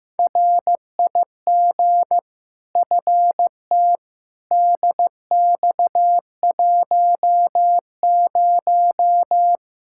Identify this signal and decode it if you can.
Morse code, heard as a keyed tone, about 15 wpm, 700 Hz. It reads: RIG FT DX10
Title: Practice QSO - Exchange 2 - Info - 15wpm